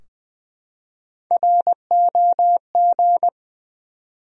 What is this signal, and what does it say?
Morse code, heard as a keyed tone, about 20 wpm, 700 Hz. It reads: ROG
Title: USA Common Names - Max 3 Letters - 20wpm